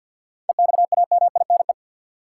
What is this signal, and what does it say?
Morse code, heard as a keyed tone, about 53 wpm, 700 Hz. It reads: EXAMINE
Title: Sentences from Top 1600 Words - New Word Drill - 50wpm